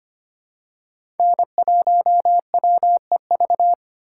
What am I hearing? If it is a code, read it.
Morse code, heard as a keyed tone, about 25 wpm, 700 Hz. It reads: N1WEV
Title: USA Call Signs - 2x Character Spacing - 25wpm